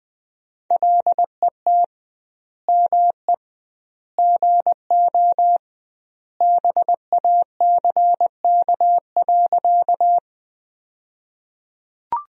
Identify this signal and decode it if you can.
Morse code, heard as a keyed tone, about 20 wpm, 700 Hz. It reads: LET ME GO BACK.
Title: 2x Word-Spacing - Sentences from Top 300 Words - 20wpm